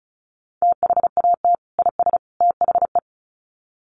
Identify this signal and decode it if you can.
Morse code, heard as a keyed tone, about 34 wpm, 700 Hz. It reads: THAT IS THE
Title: Sets of 3 Words 35wpm